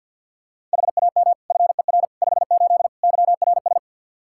Morse code, heard as a keyed tone, about 52 wpm, 700 Hz. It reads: HAM FER 58 YRS
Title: K1USN SST QSO Practice 50wpm